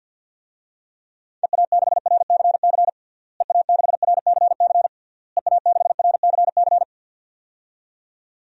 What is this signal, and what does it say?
Morse code, heard as a keyed tone, about 50 wpm, 700 Hz. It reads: EA6RCC EA6RCC EA6RCC
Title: DXCC-100 Call Signs - 6 Characters - Repeated 3x - 50wpm